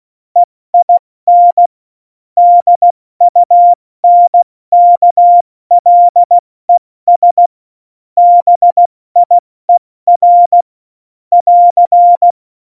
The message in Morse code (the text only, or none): EIN DUNKLES BIER <AR>